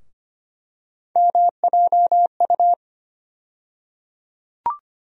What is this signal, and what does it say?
Morse code, heard as a keyed tone, about 25 wpm, 700 Hz. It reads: MJU
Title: USA Call Sign Postfixes - 3 Letters 25wpm